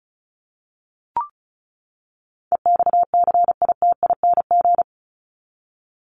Morse code, heard as a keyed tone, about 34 wpm, 700 Hz. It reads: EXCITING
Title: Top 1751-2000 Words 35wpm